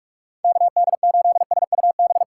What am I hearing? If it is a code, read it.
Morse code, heard as a keyed tone, about 59 wpm, spaced as for 52 wpm, 700 Hz. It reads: KD8SUB
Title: Speed Racing - USA Call Signs 45wpm